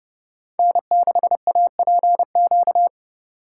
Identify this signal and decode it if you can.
Morse code, heard as a keyed tone, about 29 wpm, 700 Hz. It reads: N6APQ